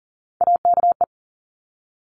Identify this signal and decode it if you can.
Morse code, heard as a keyed tone, about 38 wpm, 700 Hz. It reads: AKE
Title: Top Three Letter Combinations in Words 40wpm